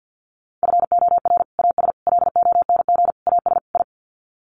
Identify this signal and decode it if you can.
Morse code, heard as a keyed tone, about 50 wpm, 700 Hz. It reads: FOR AS LONG AS I
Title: Sets of 5 Words Encore2 50wpm